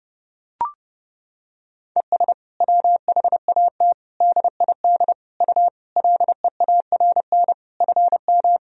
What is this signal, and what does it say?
Morse code, heard as a keyed tone, about 30 wpm, 700 Hz. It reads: ES WHAT DID U LEARN FM
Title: Sentences from Top 300 Words and Abbreviations 30wpm